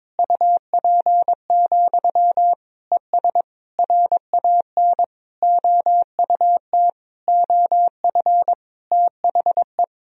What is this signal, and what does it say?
Morse code, heard as a keyed tone, about 23 wpm, 700 Hz. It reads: UP, ES RAN OUT OF THE
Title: Sentences from Top 500 Words and Abbreviations 22wpm